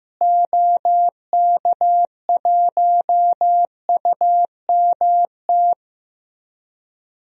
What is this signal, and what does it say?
Morse code, heard as a keyed tone, about 15 wpm, 700 Hz. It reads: OK1UMT
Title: DXCC-100 Call Signs - 6 Characters - Repeated 3x - 15wpm